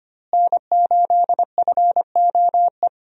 Morse code, heard as a keyed tone, about 25 wpm, 700 Hz. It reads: N8FOE
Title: USA Call Signs - 5 Characters - Repeated 2x - 25wpm